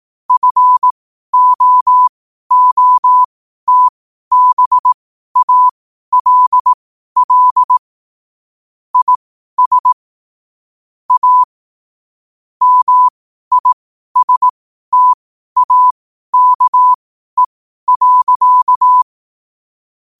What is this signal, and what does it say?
Morse code, heard as a keyed tone, about 18 wpm, spaced as for 12 wpm, 1000 Hz. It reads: FOOTBALL IS A MISTAKE.
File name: qotdfiles / 12wpm